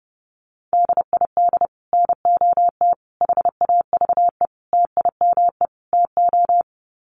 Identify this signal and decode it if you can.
Morse code, heard as a keyed tone, about 30 wpm, 700 Hz. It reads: DID NOT HAVE TIME TO
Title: Sets of 5 Words Encore 30wpm